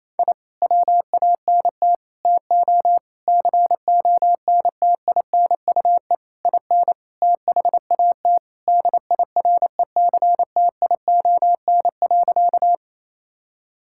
Morse code, heard as a keyed tone, about 28 wpm, 690 Hz. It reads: I WANT TO CONTINUE IN THAT DIRECTION.
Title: Sentences from Top 1000 Words - Review Part 6 - 28wpm